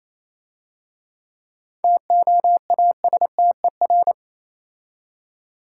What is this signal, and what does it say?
Morse code, heard as a keyed tone, about 27 wpm, 700 Hz. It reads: TOASTER
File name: Easy Words - Speed-Racing with Hints - 22wpm